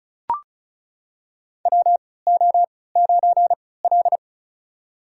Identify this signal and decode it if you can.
Morse code, heard as a keyed tone, about 34 wpm, spaced as for 20 wpm, 700 Hz. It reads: WO9L
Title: USA Call Signs - 4 Characters - 3x Character Spacing - 35wpm